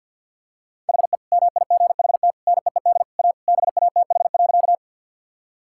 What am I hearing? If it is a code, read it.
Morse code, heard as a keyed tone, about 50 wpm, 700 Hz. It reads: HE MIGHT NEED A BATH.
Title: Sentences from Top 1600 Words 50wpm